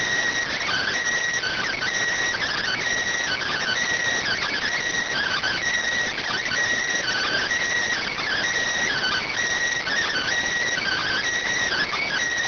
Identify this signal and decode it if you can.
Emergency Alert System (SAME) tones (header bursts), heard as harsh buzzing bursts. It is heard from 0.3 to 11.9 s.